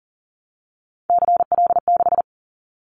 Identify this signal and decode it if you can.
Morse code, heard as a keyed tone, about 40 wpm, 700 Hz. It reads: CL6
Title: Callsign Prefixes for Top 100 DXCC - 40wpm